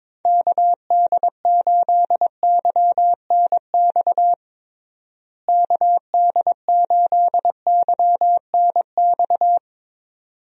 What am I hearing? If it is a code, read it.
Morse code, heard as a keyed tone, about 22 wpm, 700 Hz. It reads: KD8YNX KD8YNX
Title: USA Call Signs - 6 Characters - Repeated 2x - 22wpm